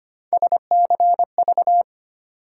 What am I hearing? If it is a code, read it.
Morse code, heard as a keyed tone, about 25 wpm, 700 Hz. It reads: SCV
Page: morse-code-ninja-practice-25wpm